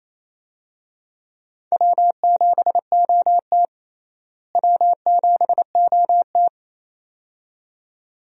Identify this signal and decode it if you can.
Morse code, heard as a keyed tone, about 28 wpm, 700 Hz. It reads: W7OT W7OT
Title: USA Call Signs - 4 Characters - Repeated 2x - 28wpm